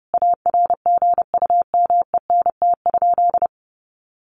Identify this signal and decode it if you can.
Morse code, heard as a keyed tone, about 30 wpm, 700 Hz. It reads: ARGUMENT?
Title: Sentences from Top 1400 Words 30wpm